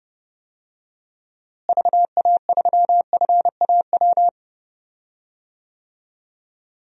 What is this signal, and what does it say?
Morse code, heard as a keyed tone, about 30 wpm, 700 Hz. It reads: VA3FAW
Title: DXCC-100 Call Signs - 6 Characters - Repeated 3x - 30wpm